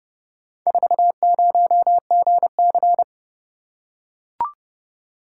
Morse code, heard as a keyed tone, about 29 wpm, 700 Hz. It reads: 40GC